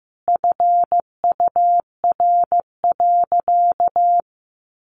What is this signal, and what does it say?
Morse code, heard as a keyed tone, about 15 wpm, 700 Hz. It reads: FUR.
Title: Easy Phrases 15wpm